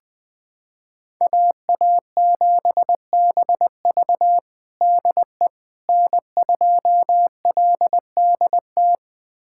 Morse code, heard as a keyed tone, about 20 wpm, 700 Hz. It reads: AA7BV DE N2LDT
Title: USA Callsign DE Callsign - 5 Characters - 20wpm